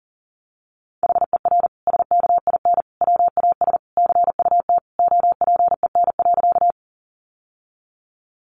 Morse code, heard as a keyed tone, about 40 wpm, 700 Hz. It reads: HER SKIN WAS CUT OPEN.
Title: Sentences from Top 1300 Words 40wpm